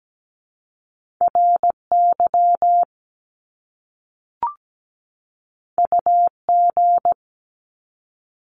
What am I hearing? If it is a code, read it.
Morse code, heard as a keyed tone, about 17 wpm, 700 Hz. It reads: RY UG
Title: Top Two Letter Combinations in Words 17wpm